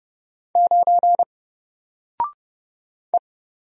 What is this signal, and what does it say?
Morse code, heard as a keyed tone, about 29 wpm, 700 Hz. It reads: 9 E
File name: Single-Letter-Number-30wpm